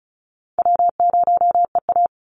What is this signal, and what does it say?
Morse code, heard as a keyed tone, about 34 wpm, 700 Hz. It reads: W0EA
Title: Call Signs Repeated 2x - 35wpm